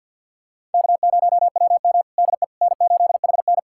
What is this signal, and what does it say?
Morse code, heard as a keyed tone, about 45 wpm, 700 Hz. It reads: K0WM DE N8HN
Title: USA Callsign DE Callsign - 4 Characters - 50wpm